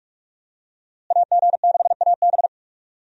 Morse code, heard as a keyed tone, about 43 wpm, 700 Hz. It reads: AG6AB
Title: USA Call Signs - 5 Characters - Repeated 3x - 45wpm